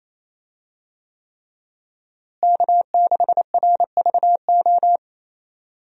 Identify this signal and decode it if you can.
Morse code, heard as a keyed tone, about 28 wpm, 700 Hz. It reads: K6RVO
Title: USA Call Signs - 5 Characters - Repeated 3x - 28wpm